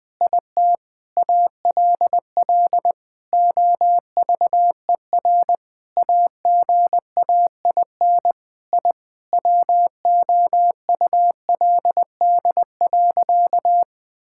Morse code, heard as a keyed tone, about 20 wpm, 700 Hz. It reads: IT ALL OVER AGAIN I WOULD.
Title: Sentences with 9-Word Phrases - Spoken Beginning - 20wpm